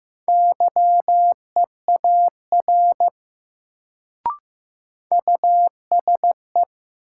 Morse code, heard as a keyed tone, about 15 wpm, 700 Hz. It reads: YEAR USE
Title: Sets of 2 Words - New Word Drill - 15wpm